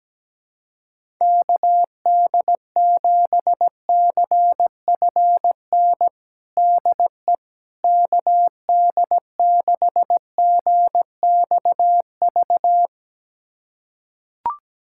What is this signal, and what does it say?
Morse code, heard as a keyed tone, about 17 wpm, 700 Hz. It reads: KD7CFN DE KD6GXV
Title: USA Callsign DE Callsign 17wpm